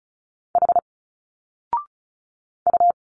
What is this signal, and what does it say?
Morse code, heard as a keyed tone, about 33 wpm, 700 Hz. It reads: H U
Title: Single Letters 35wpm